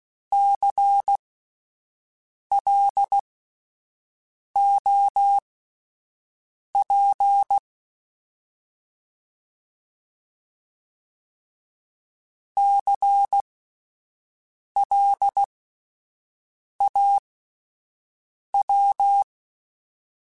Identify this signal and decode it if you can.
Morse code, heard as a keyed tone, about 16 wpm, spaced as for 5 wpm, 800 Hz. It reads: CLOP CLAW
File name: Spaced Alphas / Spaced Alphas mp3s / Cs sp @10